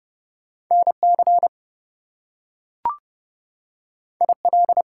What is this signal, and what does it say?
Morse code, heard as a keyed tone, about 30 wpm, 700 Hz. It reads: NC IL
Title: US States and Canadian Provinces 30wpm